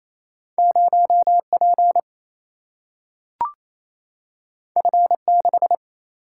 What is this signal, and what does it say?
Morse code, heard as a keyed tone, about 28 wpm, 700 Hz. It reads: 0P F6
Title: Single-Letter Single-Number 28wpm